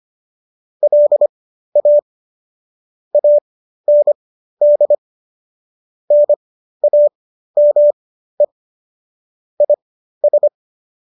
Morse code, heard as a keyed tone, about 25 wpm, spaced as for 13 wpm, 590 Hz. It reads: LA AND NAME IS